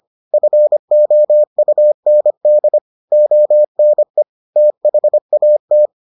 Morse code, heard as a keyed tone, about 25 wpm, 600 Hz. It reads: FOUND ONE THAT